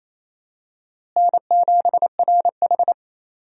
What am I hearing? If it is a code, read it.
Morse code, heard as a keyed tone, about 28 wpm, 700 Hz. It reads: N7RH
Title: USA Call Signs - 4 Characters - Repeated 3x - 28wpm